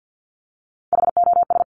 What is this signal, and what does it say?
Morse code, heard as a keyed tone, about 50 wpm, 700 Hz. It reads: HOS
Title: Top Three Letter Combinations in Words 50wpm